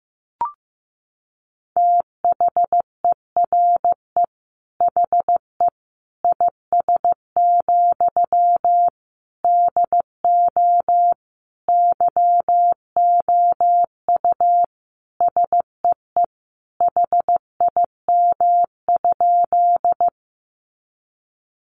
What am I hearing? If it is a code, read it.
Morse code, heard as a keyed tone, about 15 wpm, 700 Hz. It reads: THERE HE IS, DO YOU SEE HIM?
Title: Sentences from Top 100 Words 15wpm